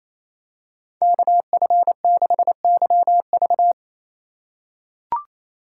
Morse code, heard as a keyed tone, about 28 wpm, 700 Hz. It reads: KF6YV
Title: USA Call Signs - 2x Character Spacing - 28wpm